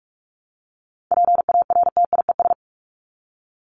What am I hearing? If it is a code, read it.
Morse code, heard as a keyed tone, about 44 wpm, 700 Hz. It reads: PARTIES
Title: ES:S Suffix - Top 100 Words with Base Words 45wpm